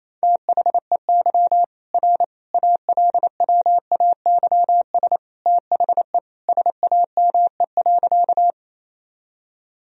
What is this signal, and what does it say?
Morse code, heard as a keyed tone, about 27 wpm, 700 Hz. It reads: THEY R ALWAYS THE SAME.
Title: Sentences from Top 300 Words and Abbreviations 28wpm